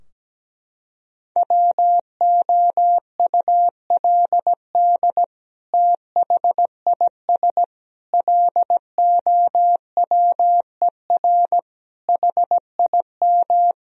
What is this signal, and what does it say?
Morse code, heard as a keyed tone, about 17 wpm, 700 Hz. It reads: WOULD THIS LOWER HIM
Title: Sentences from Top 1000 Words - Review Part 4 - 17wpm